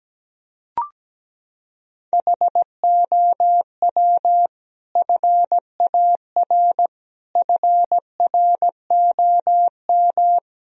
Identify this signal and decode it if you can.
Morse code, heard as a keyed tone, about 17 wpm, 700 Hz. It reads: HOW FAR FROM
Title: Sentences from Top 1000 Words - Review Part 2 - 17wpm